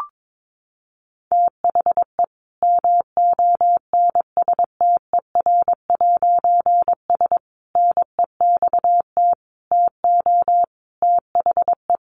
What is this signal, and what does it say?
Morse code, heard as a keyed tone, about 22 wpm, 700 Hz. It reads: THE MONSTER'S NEXT TO THE
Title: A Sound Of Thunder 22wpm